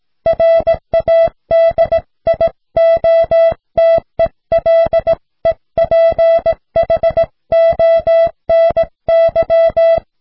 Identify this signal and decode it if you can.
Morse code, heard as a keyed tone, about 17 wpm, 660 Hz. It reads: RADIOTELEPHONY